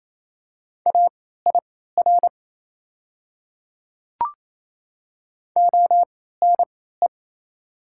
Morse code, heard as a keyed tone, about 27 wpm, spaced as for 16 wpm, 700 Hz. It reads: AIR ONE
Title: 3x Character Spacing - Top 200 Words - 28wpm